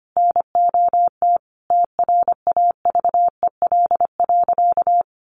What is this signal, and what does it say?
Morse code, heard as a keyed tone, about 25 wpm, 700 Hz. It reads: NOT TRAVEL.
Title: Sentences from Top 1300 Words 25wpm